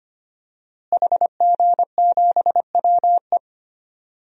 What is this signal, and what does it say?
Morse code, heard as a keyed tone, about 25 wpm, 700 Hz. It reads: HG7WE
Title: DXCC-100 Call Signs - 5 Characters - Repeated 2x - 25wpm